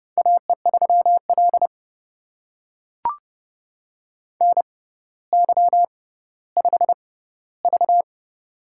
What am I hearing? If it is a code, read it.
Morse code, heard as a keyed tone, about 31 wpm, spaced as for 16 wpm, 700 Hz. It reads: AE3L NY5V
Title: USA Call Signs - 4 Characters - 6x Character Spacing - 30wpm